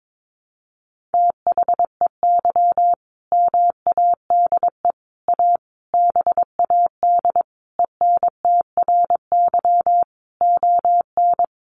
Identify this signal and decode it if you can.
Morse code, heard as a keyed tone, about 22 wpm, 700 Hz. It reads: THEY MADE A BAD ENTRY ON
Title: Sentences from Top 1500 Words 22wpm